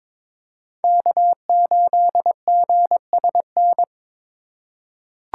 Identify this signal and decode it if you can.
Morse code, heard as a keyed tone, about 22 wpm, 700 Hz. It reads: K8GSN
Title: USA Call Signs - 5 Characters - 3x Character Spacing - 22wpm